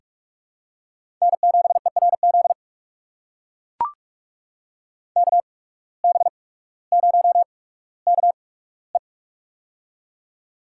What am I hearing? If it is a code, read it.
Morse code, heard as a keyed tone, about 44 wpm, spaced as for 19 wpm, 700 Hz. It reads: N7ERZ KB0KE
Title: USA Call Signs - 5 Characters - 8x Character Spacing - 45wpm